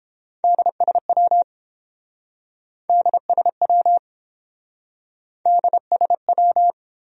Morse code, heard as a keyed tone, about 29 wpm, 700 Hz. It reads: DSW DSW DSW